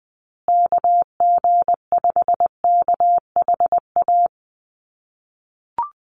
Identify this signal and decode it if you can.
Morse code, heard as a keyed tone, about 20 wpm, 700 Hz. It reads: KG5KHA
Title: USA Call Signs Repeated 3x 20wpm